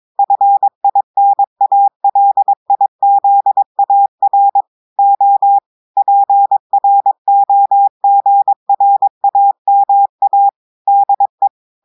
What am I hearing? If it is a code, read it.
Morse code, heard as a keyed tone, about 22 wpm, 800 Hz. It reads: FINALIZAR O PROGRAMA DE
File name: audios / portugais / portugais texte en clair 2 vitesse 22 mots minutes